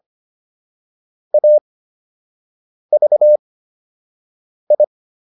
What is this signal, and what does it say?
Morse code, heard as a keyed tone, about 25 wpm, 600 Hz. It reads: A V I